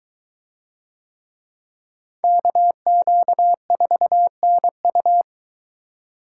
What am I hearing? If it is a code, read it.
Morse code, heard as a keyed tone, about 21 wpm, 700 Hz. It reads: KQ4NU